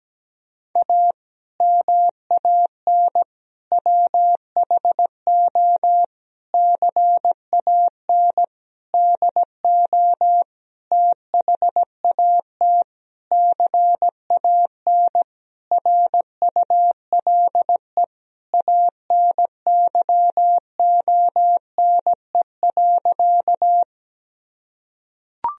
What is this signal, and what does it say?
Morse code, heard as a keyed tone, about 17 wpm, 700 Hz. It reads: A MAN WHO CAN DO THAT CAN RULE ANYONE.
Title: Sentences from Top 1000 Words - Review Part 2 - 17wpm